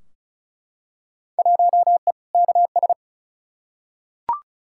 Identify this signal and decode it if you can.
Morse code, heard as a keyed tone, about 36 wpm, 700 Hz. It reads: 1E KS
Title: Field Day Exchanges - 35wpm